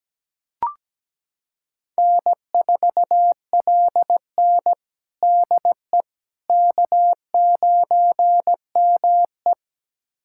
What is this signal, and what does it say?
Morse code, heard as a keyed tone, about 17 wpm, 700 Hz. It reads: N4LN DE K9ME
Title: USA Callsign DE Callsign - 4 Characters - 17wpm